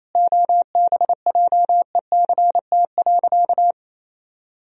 Morse code, heard as a keyed tone, about 27 wpm, 700 Hz. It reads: OBJECT.
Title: Sentences from Top 1000 Words - Review Part 3 - 28wpm